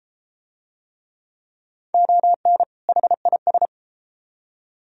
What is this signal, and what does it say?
Morse code, heard as a keyed tone, about 33 wpm, 700 Hz. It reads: ON HIS